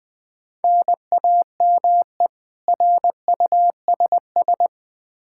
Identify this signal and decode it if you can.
Morse code, heard as a keyed tone, about 20 wpm, 700 Hz. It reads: NAME RUSS
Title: K1USN SST QSO Practice 20wpm